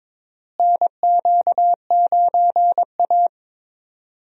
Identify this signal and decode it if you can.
Morse code, heard as a keyed tone, about 22 wpm, 700 Hz. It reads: NQ9A
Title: USA Call Signs - 4 Characters - Repeated 2x - 22wpm